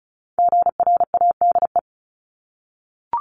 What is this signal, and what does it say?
Morse code, heard as a keyed tone, about 36 wpm, 700 Hz. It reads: GRADE
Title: Multi-letter Phonemes 35wpm